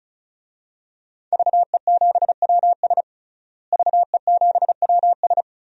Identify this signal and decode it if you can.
Morse code, heard as a keyed tone, about 36 wpm, 700 Hz. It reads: VE7WS VE7WS
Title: DXCC-100 Call Signs - 5 Characters - Repeated 2x - 35wpm